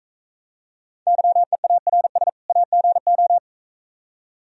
Morse code, heard as a keyed tone, about 42 wpm, 700 Hz. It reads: YEARS AGO